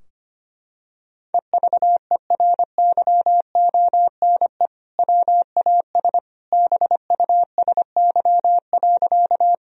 Morse code, heard as a keyed tone, about 25 wpm, 700 Hz. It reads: EVERYONE WAS BUSY.
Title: Sentences from Top 1000 Words - Review Part 3 - 25wpm